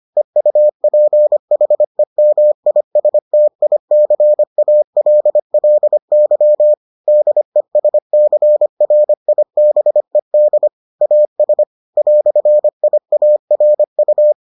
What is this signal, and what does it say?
Morse code, heard as a keyed tone, about 25 wpm, 600 Hz. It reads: EUPHEMISTICALLY DESCRIBED AS "IARU